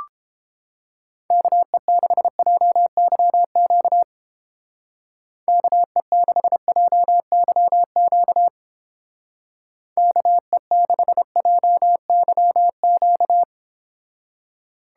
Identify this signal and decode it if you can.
Morse code, heard as a keyed tone, about 29 wpm, 700 Hz. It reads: KE6JYQ KE6JYQ KE6JYQ